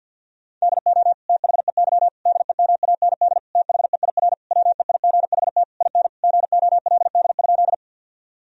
Morse code, heard as a keyed tone, about 52 wpm, 700 Hz. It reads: DO THEY DEMAND THEIR WEIGHT IN GOLD?
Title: Sentences from Top 1000 Words - Review Part 3 - 50wpm